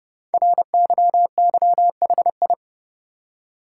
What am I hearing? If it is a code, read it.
Morse code, heard as a keyed tone, about 30 wpm, 690 Hz. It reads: RYYHI
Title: Code Groups - 5 Letters - 30wpm